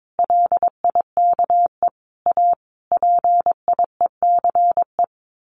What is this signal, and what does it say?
Morse code, heard as a keyed tone, about 22 wpm, 700 Hz. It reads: LIKE A PIECE
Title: Sentences from Top 1100 Words 22wpm